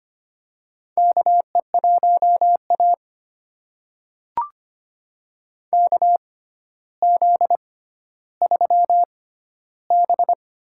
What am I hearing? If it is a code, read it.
Morse code, heard as a keyed tone, about 25 wpm, spaced as for 13 wpm, 700 Hz. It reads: KE1A KZ3B
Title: USA Call Signs - 4 Characters - 6x Character Spacing - 25wpm